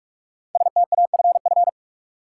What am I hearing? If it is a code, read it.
Morse code, heard as a keyed tone, about 46 wpm, 700 Hz. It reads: STAFF
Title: Sentences from Top 1200 Words - New Word Drill - 45wpm